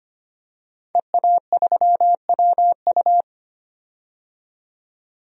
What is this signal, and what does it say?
Morse code, heard as a keyed tone, about 25 wpm, 700 Hz. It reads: EA3WU
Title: DXCC-100 Call Signs - 5 Characters - 25wpm